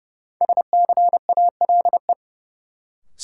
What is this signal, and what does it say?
Morse code, heard as a keyed tone, about 31 wpm, 700 Hz. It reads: SCALE
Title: Sentences from Top 1000 Words - Review Part 4 - New Word Drill - 30wpm